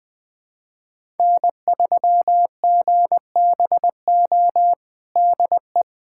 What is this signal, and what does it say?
Morse code, heard as a keyed tone, about 20 wpm, 700 Hz. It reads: N3GBO DE
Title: USA Callsign DE Callsign - 5 Characters - 20wpm